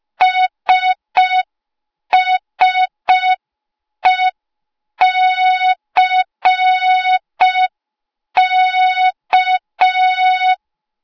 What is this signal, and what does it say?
Morse code, heard as a keyed tone, about 5 wpm, 3700 Hz. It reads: SSECK
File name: Morze-HA2MN / hang / 08KXC HA2MN